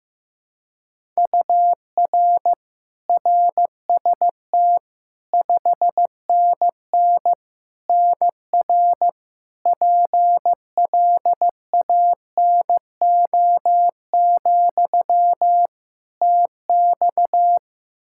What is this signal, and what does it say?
Morse code, heard as a keyed tone, about 15 wpm, 700 Hz. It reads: UR RST 5NN NR PLANO, TX
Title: Practice QSO - Exchange 1 - Beginning - 15wpm